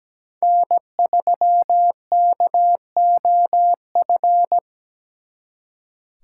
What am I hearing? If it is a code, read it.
Morse code, heard as a keyed tone, about 20 wpm, 700 Hz. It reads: N3KOF
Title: Speed Racing - USA 1x3 Call Signs 17wpm